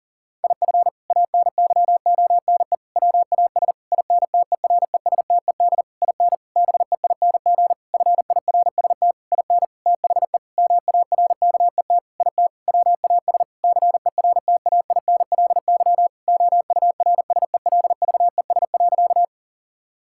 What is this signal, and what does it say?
Morse code, heard as a keyed tone, about 40 wpm, 700 Hz. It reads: IF ANYONE WAS INTERESTED IN BEING FIRST IN THE MARKET IT WAS CERTAINLY OURSELVES.
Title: Sentences from Top 1600 Words 40wpm